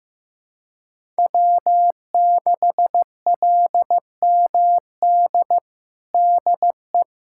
Morse code, heard as a keyed tone, about 15 wpm, 700 Hz. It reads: W6LMD DE
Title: USA Callsign DE Callsign - 5 Characters - 15wpm